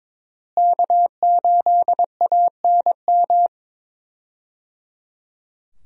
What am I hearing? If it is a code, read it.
Morse code, heard as a keyed tone, about 22 wpm, 700 Hz. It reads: K8ANM
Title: USA Call Signs - 5 Characters - Repeated 3x - 22wpm